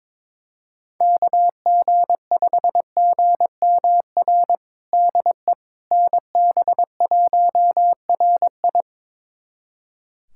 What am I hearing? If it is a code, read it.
Morse code, heard as a keyed tone, about 22 wpm, 700 Hz. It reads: KG5GMR DE NB1RI
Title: USA Callsign DE Callsign 22wpm